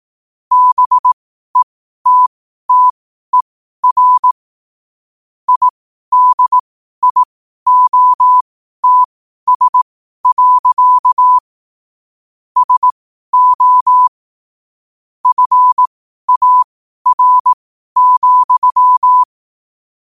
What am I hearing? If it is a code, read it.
Morse code, heard as a keyed tone, about 18 wpm, spaced as for 12 wpm, 1000 Hz. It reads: BETTER IDIOTS. SO FAR,
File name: qotdfiles / 12wpm